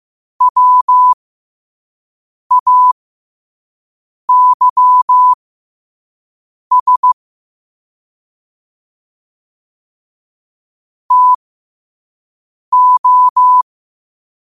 Morse code, heard as a keyed tone, about 15 wpm, spaced as for 5 wpm, 1000 Hz. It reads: WAYS TO
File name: qotdfiles / 5wpm